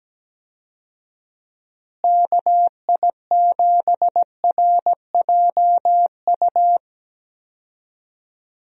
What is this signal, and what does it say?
Morse code, heard as a keyed tone, about 17 wpm, 700 Hz. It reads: KI7RJU
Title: USA Call Signs - 6 Characters - Repeated 3x - 17wpm